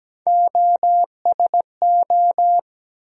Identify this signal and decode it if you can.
Morse code, heard as a keyed tone, about 17 wpm, 700 Hz. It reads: OSO
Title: USA Call Sign Postfixes - 3 Letters 17wpm